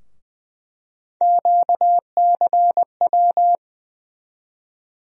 Morse code, heard as a keyed tone, about 20 wpm, 700 Hz. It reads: QCW
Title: USA Call Sign Postfixes - 3 Letters 20wpm